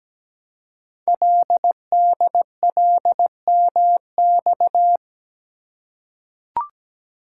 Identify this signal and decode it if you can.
Morse code, heard as a keyed tone, about 17 wpm, 700 Hz. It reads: LDLMX
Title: Code Groups - 5 Letters - 17wpm